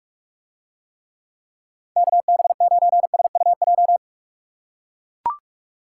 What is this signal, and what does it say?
Morse code, heard as a keyed tone, about 43 wpm, 700 Hz. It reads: KB9SUJ